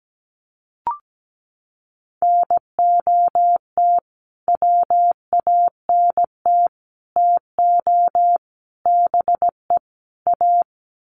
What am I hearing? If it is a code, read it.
Morse code, heard as a keyed tone, about 17 wpm, 700 Hz. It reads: NOT WANT TO BE A
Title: Sets of 5 Words Encore 17wpm